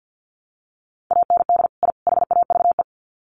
Morse code, heard as a keyed tone, about 50 wpm, 700 Hz. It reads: AND I HAVE
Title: Sets of 3 Words 50wpm